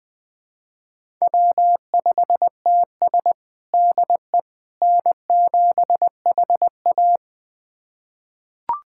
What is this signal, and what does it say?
Morse code, heard as a keyed tone, about 21 wpm, 700 Hz. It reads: W5TS DE N7HA
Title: USA Callsign DE Callsign - 4 Characters - 20wpm